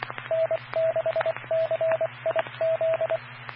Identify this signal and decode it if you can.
Morse code, heard as a keyed tone, about 24 wpm, 650 Hz. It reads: N6CIZ
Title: Claremont, CA Police, Amateur Radio